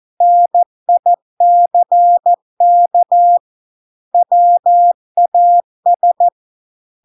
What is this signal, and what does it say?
Morse code, heard as a keyed tone, about 14 wpm, 700 Hz. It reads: NICK WAS